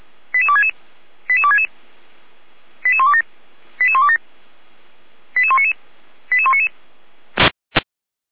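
Five-tone selective call (ZVEI), heard as quick runs of five tones, tones of about 70 ms. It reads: 88266 88266 88127 88127 88199 88199
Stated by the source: SDR receiver recording of ZVEI five-tone selective call 88125.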